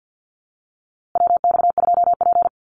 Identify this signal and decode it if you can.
Morse code, heard as a keyed tone, about 50 wpm, 700 Hz. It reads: WX2P
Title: USA Call Signs Repeated 3x 50wpm